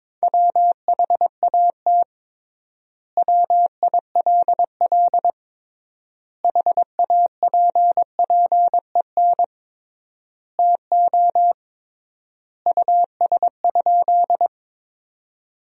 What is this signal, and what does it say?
Morse code, heard as a keyed tone, about 21 wpm, 700 Hz. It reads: WHAT WILL HAPPEN TO US?
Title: 3x Word-Spacing - Sentences from Top 400 Words - 22wpm